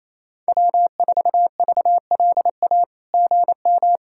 Morse code, heard as a keyed tone, about 28 wpm, 700 Hz. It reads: W4VLA GM
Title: SOTA QSO Practice 28wpm